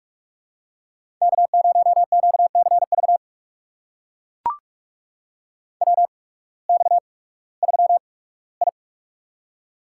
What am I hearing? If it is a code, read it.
Morse code, heard as a keyed tone, about 44 wpm, spaced as for 21 wpm, 700 Hz. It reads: K0QCV WX3I
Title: USA Call Signs - 8x Character Spacing - 45wpm